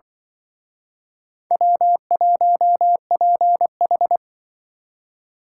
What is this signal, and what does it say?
Morse code, heard as a keyed tone, about 24 wpm, 700 Hz. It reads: W1PH
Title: Speed Racing - USA 1x2 Call Signs 20wpm